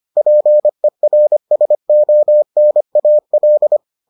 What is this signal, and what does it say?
Morse code, heard as a keyed tone, about 25 wpm, 600 Hz. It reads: PERSONAL